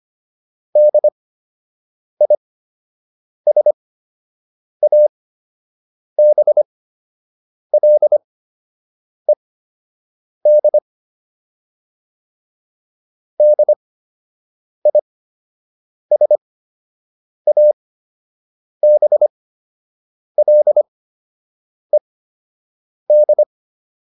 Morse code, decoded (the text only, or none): DISABLED DISABLED